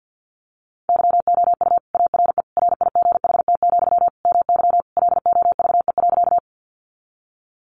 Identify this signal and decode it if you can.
Morse code, heard as a keyed tone, about 49 wpm, 700 Hz. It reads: YOU ARE RIGHT, MY LOVE.
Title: Sentences from Top 400 Words 50wpm